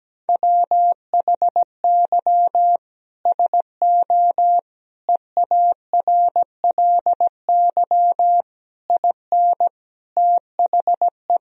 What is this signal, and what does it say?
Morse code, heard as a keyed tone, about 17 wpm, 700 Hz. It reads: WHY SO EARLY IN THE
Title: Sentences from Top 1000 Words - Review Part 2 - 17wpm